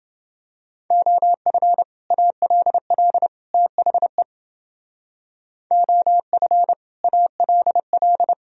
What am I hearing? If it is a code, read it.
Morse code, heard as a keyed tone, about 29 wpm, 700 Hz. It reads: OF ALL THE OF ALL